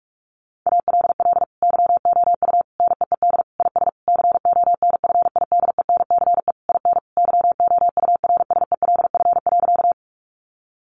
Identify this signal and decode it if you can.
Morse code, heard as a keyed tone, about 46 wpm, 700 Hz. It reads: ALL YOU NEED IS CONFIDENCE IN YOURSELF.
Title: Sentences from Top 1500 Words 45wpm